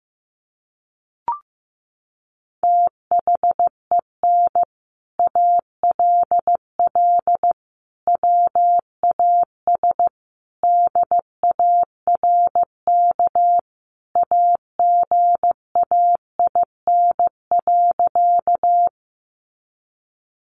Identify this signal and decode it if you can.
Morse code, heard as a keyed tone, about 15 wpm, 700 Hz. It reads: THEN ALL WAS DARK AGAIN.
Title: Sentences from Top 500 Words 15wpm